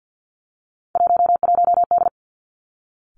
Morse code, heard as a keyed tone, about 50 wpm, 700 Hz. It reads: 11D
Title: Three Challenging Letter-Number Combinations 50wpm